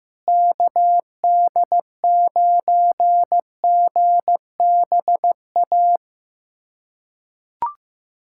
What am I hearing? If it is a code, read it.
Morse code, heard as a keyed tone, about 15 wpm, 700 Hz. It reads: KD9GBA
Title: USA Callsign DE Callsign 15wpm